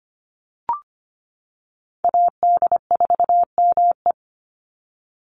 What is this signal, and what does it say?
Morse code, heard as a keyed tone, about 25 wpm, 700 Hz. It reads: AD4ME
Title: USA Call Signs 25wpm